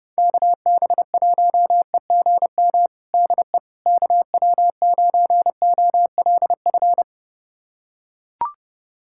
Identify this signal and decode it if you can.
Morse code, heard as a keyed tone, about 32 wpm, 690 Hz. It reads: KB1EGM DE KW9OLF
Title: USA Callsign DE Callsign 30wpm